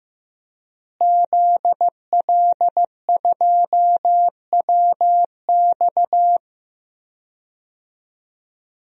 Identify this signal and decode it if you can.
Morse code, heard as a keyed tone, about 15 wpm, 700 Hz. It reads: ZL2WX
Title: DXCC-100 Call Signs - 5 Characters - Repeated 2x - 15wpm